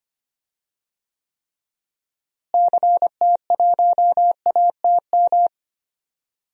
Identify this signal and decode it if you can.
Morse code, heard as a keyed tone, about 25 wpm, 700 Hz. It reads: CT1ATM
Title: DXCC-100 Call Signs - 6 Characters - Repeated 2x - 25wpm